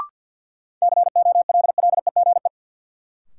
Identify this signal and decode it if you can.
Morse code, heard as a keyed tone, about 50 wpm, 700 Hz. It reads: COLLEGE
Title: Sentences from Top 1300 Words - New Word Drill - 50wpm